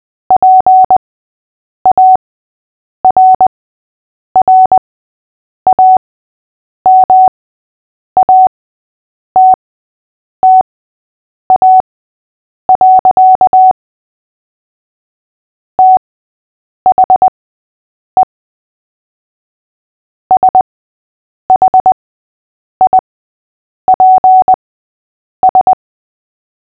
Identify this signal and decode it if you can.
Morse code, heard as a keyed tone, about 20 wpm, spaced as for 8 wpm, 750 Hz. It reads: PARRAMATTA. THE SHIPS
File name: cw / fists / Short-Story-3-8wpm